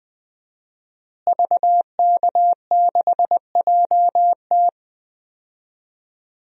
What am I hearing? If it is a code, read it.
Morse code, heard as a keyed tone, about 20 wpm, 700 Hz. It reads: VK6JT
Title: DXCC-100 Call Signs - 5 Characters - Repeated 2x - 20wpm